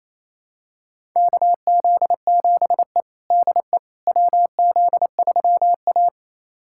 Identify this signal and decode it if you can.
Morse code, heard as a keyed tone, about 28 wpm, 700 Hz. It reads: KZ7E DE WZ3A
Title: USA Callsign DE Callsign - 4 Characters - 28wpm